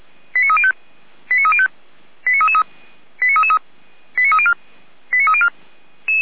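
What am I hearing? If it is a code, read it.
Five-tone selective call (ZVEI), heard as quick runs of five tones, tones of about 80 ms. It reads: 89305 89305 89333 89333 89304 89304
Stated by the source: ZVEI five-tone selective call received by SDR.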